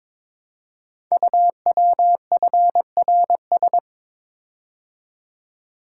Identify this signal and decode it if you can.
Morse code, heard as a keyed tone, about 22 wpm, 700 Hz. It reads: UWFRS
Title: Code Groups - 5 Letters - 22wpm